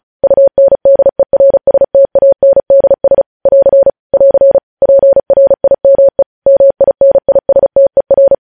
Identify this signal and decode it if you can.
Morse code, heard as a keyed tone, about 36 wpm, 550 Hz. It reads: UNDERSTANDS <AR> <AR> PRIME MINISTER
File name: podcast / News headlines-1x-35x35